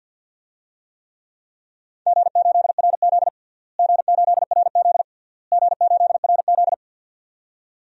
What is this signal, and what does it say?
Morse code, heard as a keyed tone, about 52 wpm, 700 Hz. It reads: G8RZ G8RZ G8RZ
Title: DXCC-100 Call Signs - 4 Characters - Repeated 3x - 50wpm